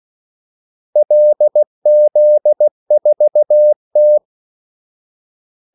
Morse code, heard as a keyed tone, about 16 wpm, 600 Hz. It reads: LZ4T
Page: Kaldesignaler 16 wpm